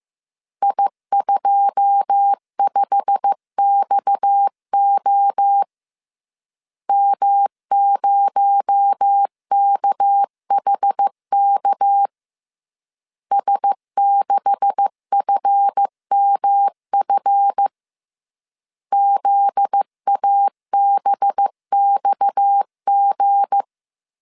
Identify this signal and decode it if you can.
Morse code, heard as a keyed tone, about 15 wpm, 800 Hz. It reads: I25XO M0KHK S6FMF ZABXG
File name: CW / DET / 1 9 1500C S C 14wpm 3245